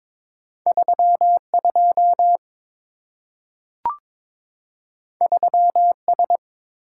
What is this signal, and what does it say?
Morse code, heard as a keyed tone, about 22 wpm, 690 Hz. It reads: 32 3S
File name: practice / 22wpm / Two Letter-Number Combinations - 22wpm